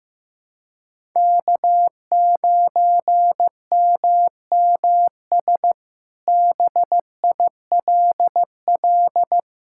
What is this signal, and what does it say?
Morse code, heard as a keyed tone, about 15 wpm, 700 Hz. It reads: K9MMS BILL
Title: K1USN SST QSO Practice - 15wpm